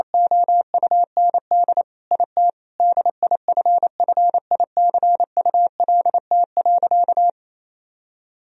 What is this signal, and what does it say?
Morse code, heard as a keyed tone, about 28 wpm, 700 Hz. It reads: OUND IT DIFFICULT.
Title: Sentences from Top 1000 Words - Review Part 3 - 28wpm